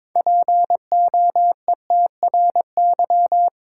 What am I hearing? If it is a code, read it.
Morse code, heard as a keyed tone, about 22 wpm, 700 Hz. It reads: POETRY
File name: Difficult Words with Hints - 22wpm